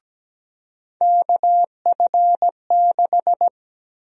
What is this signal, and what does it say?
Morse code, heard as a keyed tone, about 17 wpm, 700 Hz. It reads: KF6
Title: USA Call Sign Prefixes - 17wpm